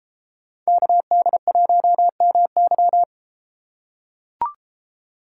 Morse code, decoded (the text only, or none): KD1MY